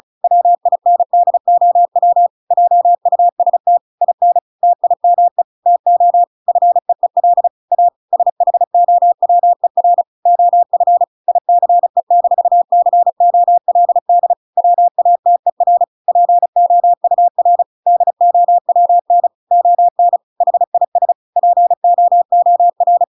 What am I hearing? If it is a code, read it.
Morse code, heard as a keyed tone, about 37 wpm, 700 Hz. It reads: WINDOW JUST IN TIME TO FEEL A SHOWER OF ICE-COLD WATER POUR DOWN ON HIS POOR